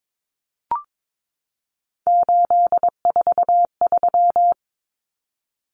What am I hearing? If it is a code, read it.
Morse code, heard as a keyed tone, about 22 wpm, 700 Hz. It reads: 843
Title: Common Cut Numbers - 3-Digit 22wpm